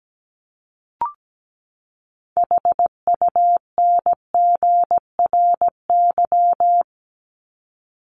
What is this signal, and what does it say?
Morse code, heard as a keyed tone, about 17 wpm, 700 Hz. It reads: HUNGRY
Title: Top 1251-1500 Words 17wpm